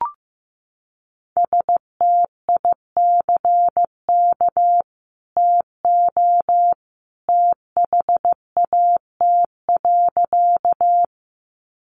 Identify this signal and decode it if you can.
Morse code, heard as a keyed tone, about 15 wpm, 700 Hz. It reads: STICK TO THAT.
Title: Sentences from Top 1200 Words 15wpm